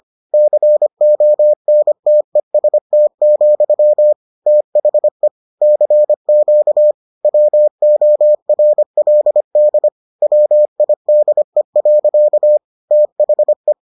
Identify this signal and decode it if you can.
Morse code, heard as a keyed tone, about 25 wpm, 600 Hz. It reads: CONTEST, THE CQ WORLD WIDE. THE